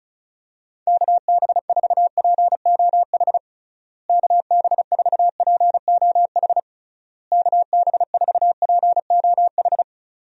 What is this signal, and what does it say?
Morse code, heard as a keyed tone, about 36 wpm, 700 Hz. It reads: KB4POH KB4POH KB4POH
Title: USA Call Signs - 6 Characters - Repeated 3x - 35wpm